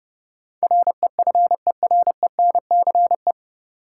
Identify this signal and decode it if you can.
Morse code, heard as a keyed tone, about 32 wpm, 700 Hz. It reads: REFERENCE
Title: Sentences from Top 1200 Words - New Word Drill - 30wpm